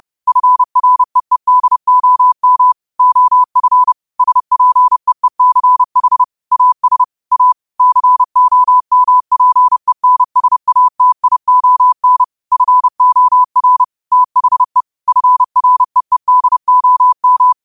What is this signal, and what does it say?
Morse code, heard as a keyed tone, about 30 wpm, 1000 Hz. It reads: FREEDOM OF SPEECH AS A COMPENSATION FOR THE FREEDOM
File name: qotdfiles / 30wpm